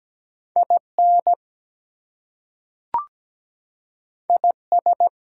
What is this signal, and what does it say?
Morse code, heard as a keyed tone, about 17 wpm, 700 Hz. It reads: IN IS
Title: Two Letter Words - 17wpm